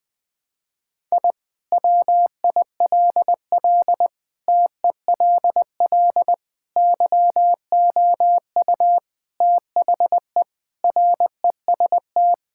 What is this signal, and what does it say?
Morse code, heard as a keyed tone, about 21 wpm, 700 Hz. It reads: I WILL TELL YOU THE REST
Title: Sentences with 4-Word Phrases - 20wpm